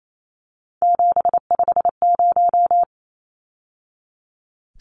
Morse code, heard as a keyed tone, about 28 wpm, 700 Hz. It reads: 750
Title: Numbers - 3-Digit 28wpm